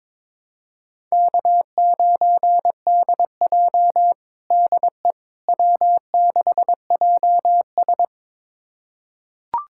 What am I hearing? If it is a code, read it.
Morse code, heard as a keyed tone, about 22 wpm, 700 Hz. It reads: K9DJ DE W6JS
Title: USA Callsign DE Callsign - 4 Characters - 22wpm